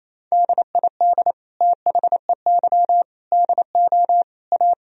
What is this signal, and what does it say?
Morse code, heard as a keyed tone, about 28 wpm, 700 Hz. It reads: DID THEY DO A
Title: Sentences from Top 1000 Words - Review Part 3 - 28wpm